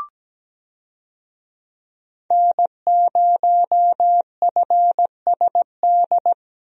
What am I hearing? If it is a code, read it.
Morse code, heard as a keyed tone, about 17 wpm, 700 Hz. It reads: N0FSD
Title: USA Call Signs - 5 Characters - Repeated 3x - 17wpm